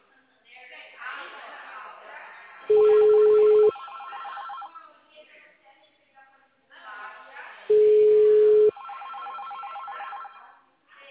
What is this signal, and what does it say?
Ringback tone (Europe, 425 Hz), with speech.